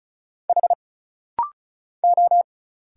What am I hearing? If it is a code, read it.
Morse code, heard as a keyed tone, about 34 wpm, 700 Hz. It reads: H O
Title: Single Letters - Rapid-Fire 35wpm